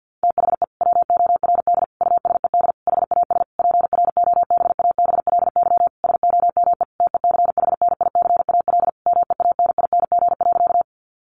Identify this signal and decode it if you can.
Morse code, heard as a keyed tone, about 50 wpm, 700 Hz. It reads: THE WORD USED HAS PROBABLY SOME TECHNICAL MEANING.
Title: Sentences from Top 1100 Words 50wpm